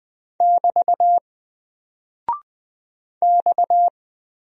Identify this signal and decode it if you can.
Morse code, heard as a keyed tone, about 20 wpm, 700 Hz. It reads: <BT> X
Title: Single Character (Letter:Number:Punctuation:Prosign) 20wpm